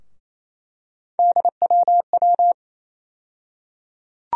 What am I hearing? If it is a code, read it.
Morse code, heard as a keyed tone, about 28 wpm, 700 Hz. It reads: DWW
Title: USA Call Sign Postfixes - 3 Letters 28wpm